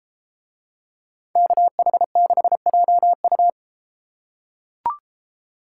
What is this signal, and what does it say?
Morse code, heard as a keyed tone, about 31 wpm, 700 Hz. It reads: KH6JU